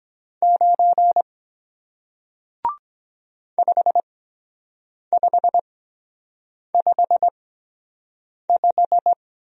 Morse code, heard as a keyed tone, about 23 wpm, spaced as for 7 wpm, 700 Hz. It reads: 9 5555